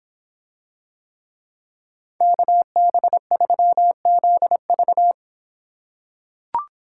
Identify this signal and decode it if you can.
Morse code, heard as a keyed tone, about 26 wpm, 700 Hz. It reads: KB3ZV